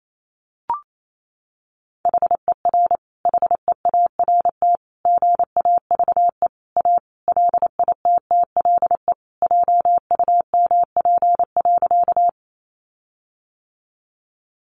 Morse code, heard as a keyed tone, about 28 wpm, 700 Hz. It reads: HER HEART GAVE A LITTLE JUMP.
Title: Sentences from Top 1300 Words 28wpm